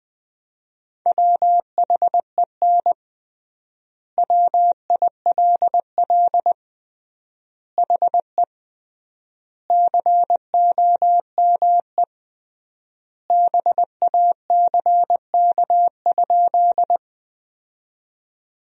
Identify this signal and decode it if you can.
Morse code, heard as a keyed tone, about 20 wpm, 700 Hz. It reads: WHEN WILL HE COME BACK?
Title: 3x Word-Spacing - Sentences from Top 200 Words - 20wpm